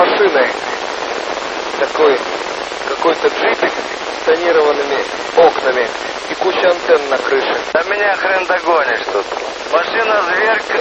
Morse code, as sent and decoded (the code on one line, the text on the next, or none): none